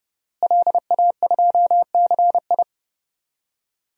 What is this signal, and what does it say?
Morse code, heard as a keyed tone, about 30 wpm, 700 Hz. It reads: LA2CI
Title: DXCC-100 Call Signs - 5 Characters - Repeated 2x - 30wpm